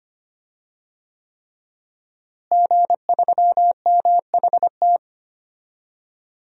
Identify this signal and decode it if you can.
Morse code, heard as a keyed tone, about 25 wpm, 700 Hz. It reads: G3MHT
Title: DXCC-100 Call Signs - 5 Characters - Repeated 3x - 25wpm